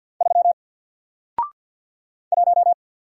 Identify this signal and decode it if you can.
Morse code, heard as a keyed tone, about 48 wpm, 700 Hz. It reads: 3 1
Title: Single Character (Letter:Number:Punctuation:Prosign) Alt 50wpm